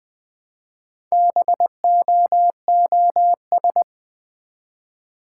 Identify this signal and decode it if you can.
Morse code, heard as a keyed tone, about 20 wpm, 700 Hz. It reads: BOOS
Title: Build A Word - 1 Letter At A Time - 5-Letter Words - 20wpm